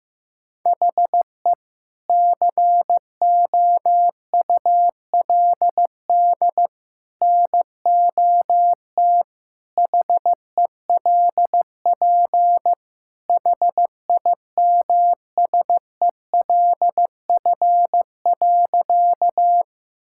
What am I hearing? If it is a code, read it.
Morse code, heard as a keyed tone, about 16 wpm, 700 Hz. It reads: HE COULD NOT HELP HIMSELF.
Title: Sentences from Top 1600 Words 15wpm